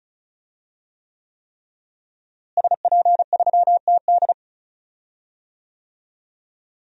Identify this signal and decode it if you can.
Morse code, heard as a keyed tone, about 34 wpm, 700 Hz. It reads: SP3TD
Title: DXCC-100 Call Signs - 5 Characters - Repeated 3x - 35wpm